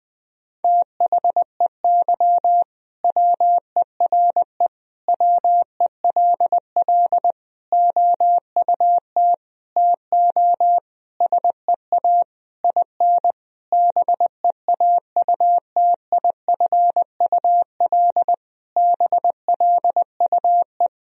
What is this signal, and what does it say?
Morse code, heard as a keyed tone, about 20 wpm, 700 Hz. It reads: THEY WERE WELL OUT TO SEA IN BEAUTIFUL BLUE
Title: Sentences from Top 1000 Words - Review Part 3 - 20wpm